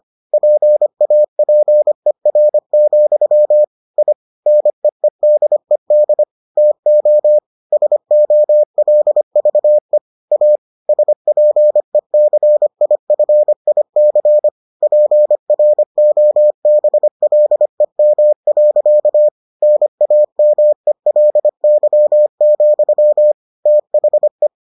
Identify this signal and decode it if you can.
Morse code, heard as a keyed tone, about 25 wpm, 600 Hz. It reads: PAPER, I NEEDED TO SOLVE A SPECIFIC PROBLEM. NAMELY, THE